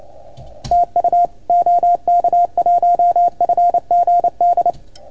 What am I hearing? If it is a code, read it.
Morse code, heard as a keyed tone, about 28 wpm, 690 Hz. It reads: TU OK1FGD
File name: Archiv / CUC310 / OK1FGD CUC 310 pileup